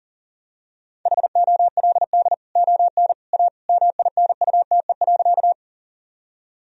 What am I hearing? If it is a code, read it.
Morse code, heard as a keyed tone, about 40 wpm, 700 Hz. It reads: HOLD ON A MINUTE.
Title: Sentences from Top 1000 Words - Review Part 2 - 40wpm